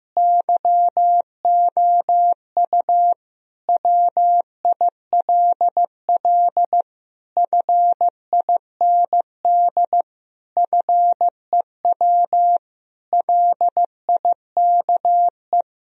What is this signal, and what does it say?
Morse code, heard as a keyed tone, about 15 wpm, 700 Hz. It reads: YOU WILL FIND FEW LIKE
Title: Sentences from Top 1000 Words - Review Part 1 - 15wpm